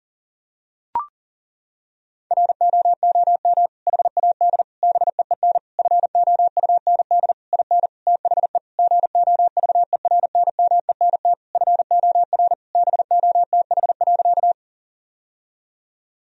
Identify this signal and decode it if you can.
Morse code, heard as a keyed tone, about 40 wpm, 700 Hz. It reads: ROOM HAD BEEN FOUND IN THE GOVERNMENT FOR BOTH.
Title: Sentences from Top 1000 Words - Review Part 3 - 40wpm